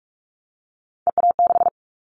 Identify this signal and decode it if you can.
Morse code, heard as a keyed tone, about 46 wpm, 700 Hz. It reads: EA6
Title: Callsign Prefixes for Top 50 DXCC - 45wpm